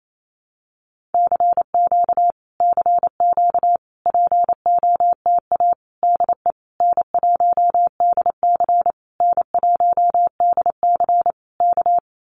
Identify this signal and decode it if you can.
Morse code, heard as a keyed tone, about 28 wpm, 700 Hz. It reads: CQ CQ POTA DE N1DC N1DC K
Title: Common QSO Phrases Encore 28wpm